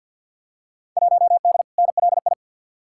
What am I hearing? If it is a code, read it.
Morse code, heard as a keyed tone, about 50 wpm, 700 Hz. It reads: 1D NLI
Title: Field Day Exchanges - 50wpm